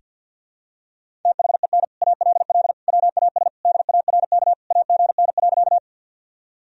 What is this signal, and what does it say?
Morse code, heard as a keyed tone, about 48 wpm, 700 Hz. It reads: THEN ALL WAS DARK AGN.
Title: Sentences from Top 500 Words and Abbreviations 50wpm